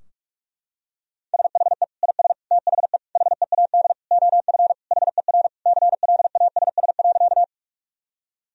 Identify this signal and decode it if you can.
Morse code, heard as a keyed tone, about 41 wpm, 700 Hz. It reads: SHE IS THE HEAD OF HER CLASS.
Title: Sentences from Top 1000 Words - Review Part 2 - 45wpm